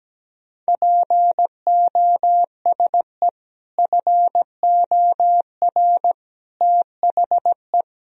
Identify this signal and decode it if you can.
Morse code, heard as a keyed tone, about 17 wpm, 700 Hz. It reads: POSE FOR THE
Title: Sentences from Top 1000 Words - Review Part 2 - 17wpm